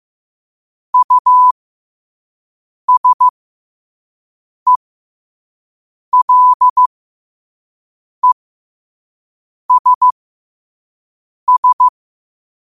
Morse code, heard as a keyed tone, about 15 wpm, spaced as for 5 wpm, 1000 Hz. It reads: USELESS
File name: qotdfiles / 5wpm